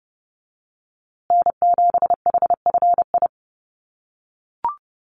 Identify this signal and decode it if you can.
Morse code, heard as a keyed tone, about 30 wpm, 700 Hz. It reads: N7HFI
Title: USA Call Signs Repeated 2x 30wpm